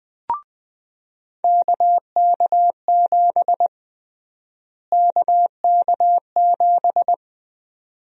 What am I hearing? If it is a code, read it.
Morse code, heard as a keyed tone, about 20 wpm, 700 Hz. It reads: KK7 KK7
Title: USA Call Sign Prefixes - Repeated 2x - 20wpm